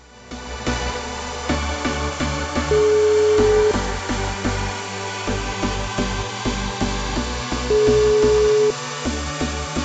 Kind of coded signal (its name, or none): telephone call progress tone